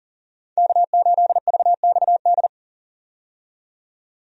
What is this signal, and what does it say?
Morse code, heard as a keyed tone, about 39 wpm, 700 Hz. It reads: K8VXD
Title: USA Call Signs - 5 Characters - 40wpm